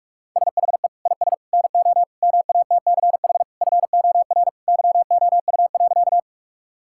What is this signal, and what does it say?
Morse code, heard as a keyed tone, about 46 wpm, 700 Hz. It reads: SHE IS NO MATCH FOR YOU.
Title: Sentences from Top 1600 Words 45wpm